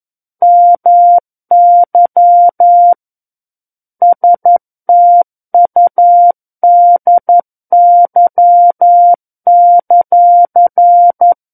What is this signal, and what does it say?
Morse code, heard as a keyed tone, about 11 wpm, 700 Hz. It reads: MY STUDY;
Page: War of the Worlds - 02-Chapter 2 - 11 WPM